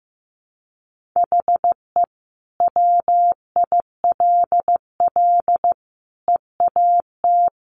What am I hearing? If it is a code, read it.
Morse code, heard as a keyed tone, about 15 wpm, 700 Hz. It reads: HE WILL EAT
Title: Sentences from Top 1000 Words 15wpm